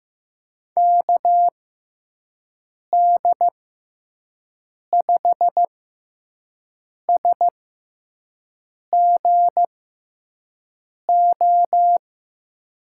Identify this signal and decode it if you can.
Morse code, heard as a keyed tone, about 15 wpm, spaced as for 5 wpm, 700 Hz. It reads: KD5SGO
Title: USA Call Signs - 6 Characters - 6x Character Spacing - 15wpm